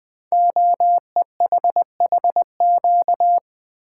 Morse code, heard as a keyed tone, about 20 wpm, 700 Hz. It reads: OEHHQ
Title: Code Groups - 5 Letters - 20wpm